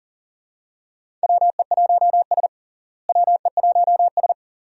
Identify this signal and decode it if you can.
Morse code, heard as a keyed tone, about 37 wpm, 700 Hz. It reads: WE1S WE1S
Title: USA Call Signs - 4 Characters - Repeated 2x - 40wpm